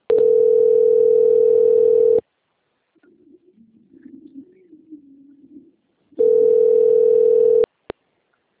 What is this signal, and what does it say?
Ringback tone (North America), 440 + 480 Hz.